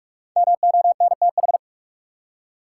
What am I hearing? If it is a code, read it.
Morse code, heard as a keyed tone, about 45 wpm, 700 Hz. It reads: MONTH
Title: Five Letter Words - 45wpm